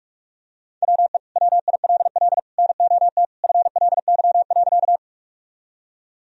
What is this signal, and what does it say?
Morse code, heard as a keyed tone, about 46 wpm, 700 Hz. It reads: WE WILL NOT FLY.
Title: Sentences from Top 1000 Words - Review Part 2 - 45wpm